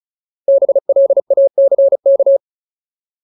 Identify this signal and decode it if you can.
Morse code, heard as a keyed tone, about 34 wpm, 550 Hz. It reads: BLACK
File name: Ultimate Word List / 35WPM Ultimate